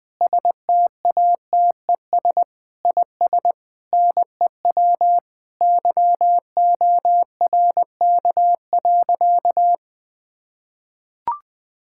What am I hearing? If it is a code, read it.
Morse code, heard as a keyed tone, about 20 wpm, 700 Hz. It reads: STATES IS NEW YORK.
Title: Sentences with 5-Word Phrases - Spoken Beginning - 20wpm